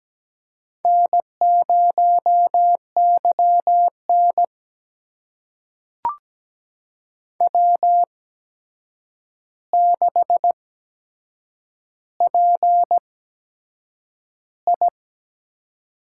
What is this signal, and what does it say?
Morse code, heard as a keyed tone, about 17 wpm, spaced as for 7 wpm, 700 Hz. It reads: N0YN W6PI
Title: USA Call Signs - 4 Characters - 8x Character Spacing - 17wpm